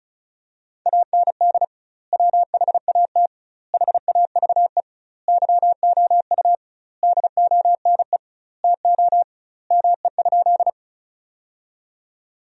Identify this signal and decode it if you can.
Morse code, heard as a keyed tone, about 36 wpm, 700 Hz. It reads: AND WHAT HAVE YOU DONE TO ME?
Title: 2x Word-Spacing - Sentences from Top 500 Words - 35wpm